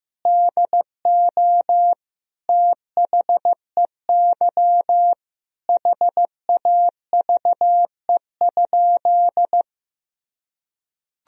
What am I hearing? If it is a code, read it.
Morse code, heard as a keyed tone, about 15 wpm, 700 Hz. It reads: DO THEY HAVE?
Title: Sentences from Top 1000 Words - Review Part 3 - 15wpm